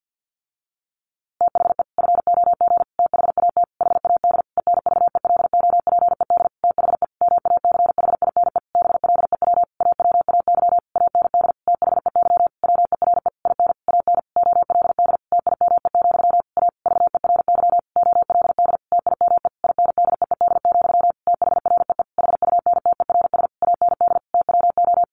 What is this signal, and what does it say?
Morse code, heard as a keyed tone, about 50 wpm, 700 Hz. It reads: THE FOG THAT HAD ENVELOPED THE MACHINE BLEW AWAY AND THEY WERE IN AN OLD TIME, A VERY OLD TIME INDEED, THREE HUNTERS AND TWO
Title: A Sound Of Thunder 50wpm